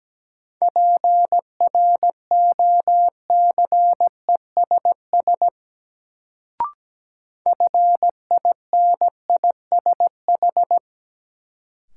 Morse code, heard as a keyed tone, about 17 wpm, 700 Hz. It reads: PROCESS FINISH
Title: Sentences from Top 1000 Words - Review Part 3 - New Word Drill - 17wpm